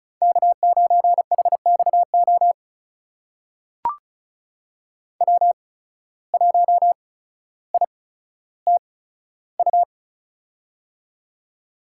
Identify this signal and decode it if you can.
Morse code, heard as a keyed tone, about 36 wpm, spaced as for 15 wpm, 700 Hz. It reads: K9HXO W1ITU
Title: USA Call Signs - 5 Characters - 8x Character Spacing - 35wpm